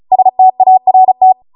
Morse code, heard as a keyed tone, about 36 wpm, 750 Hz. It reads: START